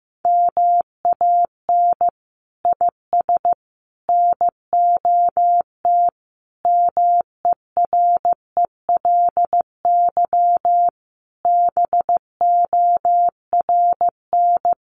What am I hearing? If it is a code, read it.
Morse code, heard as a keyed tone, about 15 wpm, 700 Hz. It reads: MAN IS NOT MERELY BORN
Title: Sentences from Top 1400 Words 15wpm